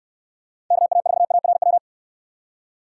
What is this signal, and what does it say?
Morse code, heard as a keyed tone, about 67 wpm, 700 Hz. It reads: KA4ARW